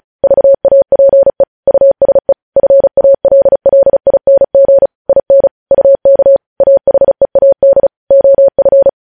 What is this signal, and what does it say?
Morse code, heard as a keyed tone, about 36 wpm, 550 Hz. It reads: VAPE USE FALLING IN UK AHEAD OF
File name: podcast / News headlines-1x-35x35